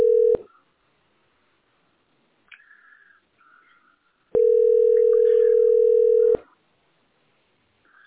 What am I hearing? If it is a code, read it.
Ringback tone (North America), 440 + 480 Hz.